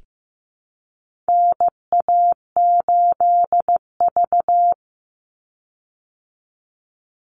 Morse code, heard as a keyed tone, about 15 wpm, 700 Hz. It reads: NA8V
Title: Call Signs Repeated 3x - 15wpm